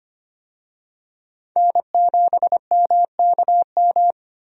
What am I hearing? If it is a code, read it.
Morse code, heard as a keyed tone, about 25 wpm, 700 Hz. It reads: N7MKM
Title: USA Call Signs - 5 Characters - Repeated 2x - 25wpm